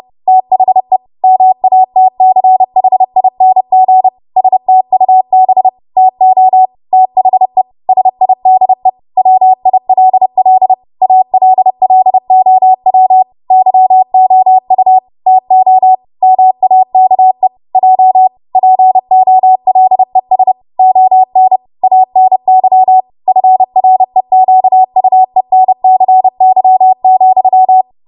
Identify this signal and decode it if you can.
Morse code, heard as a keyed tone, about 30 wpm, 750 Hz. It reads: THE MATCHING STUB TO THE SIDE WILL ALLOW YOU TO MAKE J POLES ON ANY FREQUENCY,
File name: Morse / 30WPM